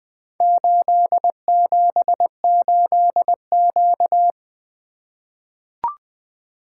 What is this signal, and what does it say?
Morse code, heard as a keyed tone, about 21 wpm, 700 Hz. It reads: 878Q
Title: Four Letter-Number Combinations - 20wpm